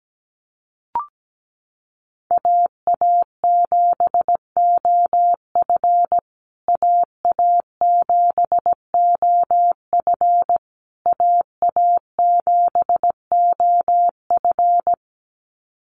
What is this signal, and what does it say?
Morse code, heard as a keyed tone, about 17 wpm, 700 Hz. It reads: AA7OF AA7OF AA7OF
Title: USA Call Signs Repeated 3x 17wpm